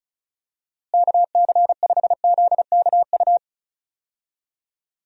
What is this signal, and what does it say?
Morse code, heard as a keyed tone, about 34 wpm, 700 Hz. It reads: KC5ZKU
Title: USA Call Signs - 6 Characters - Repeated 3x - 35wpm